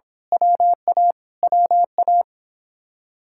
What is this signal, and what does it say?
Morse code, heard as a keyed tone, about 24 wpm, 700 Hz. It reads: WA WA
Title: Speed-Racing Common QSO Phrases - 22wpm